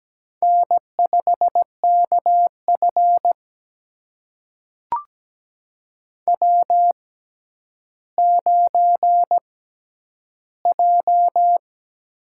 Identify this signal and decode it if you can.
Morse code, heard as a keyed tone, about 17 wpm, spaced as for 10 wpm, 700 Hz. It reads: N5KF W9J
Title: USA Call Signs - 4 Characters - 6x Character Spacing - 17wpm